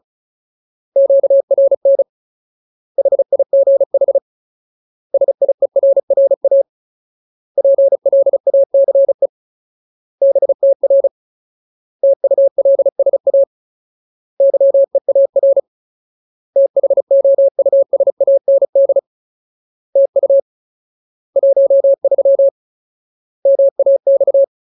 Morse code, heard as a keyed tone, about 34 wpm, 550 Hz. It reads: QRN HIGH SIERRA PLACE BTR TULSA YEAR THOUSAND TU 13 MAX
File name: Ultimate Word List / 35WPM Ultimate